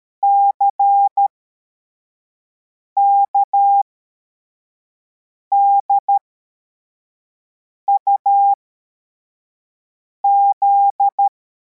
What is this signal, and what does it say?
Morse code, heard as a keyed tone, about 13 wpm, spaced as for 4 wpm, 800 Hz. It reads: CKDUZ
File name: formation / cw / Text81